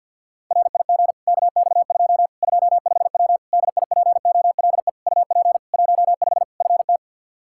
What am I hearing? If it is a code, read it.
Morse code, heard as a keyed tone, about 52 wpm, 700 Hz. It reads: RIG KX2 25W DIPOLE UP 15 FT
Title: Practice QSO - Exchange 2 - Info - 50wpm